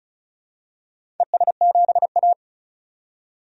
Morse code, heard as a keyed tone, about 36 wpm, 700 Hz. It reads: ES7A
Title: DXCC-100 Call Signs - 4 Characters - 35wpm